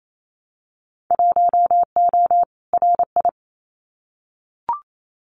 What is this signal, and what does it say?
Morse code, heard as a keyed tone, about 28 wpm, 700 Hz. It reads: 1O RI
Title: Winter Field Day Exchanges 28wpm